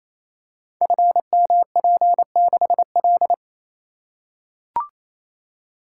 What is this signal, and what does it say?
Morse code, heard as a keyed tone, about 28 wpm, 700 Hz. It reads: FMP6L
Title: Code Groups - 5 Letter:Numbers - 28wpm